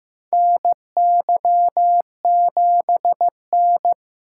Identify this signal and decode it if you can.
Morse code, heard as a keyed tone, about 15 wpm, 700 Hz. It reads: NY7N
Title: K1USN SST QSO Practice - 15wpm